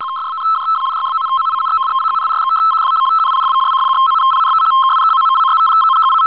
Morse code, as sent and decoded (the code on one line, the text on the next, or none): none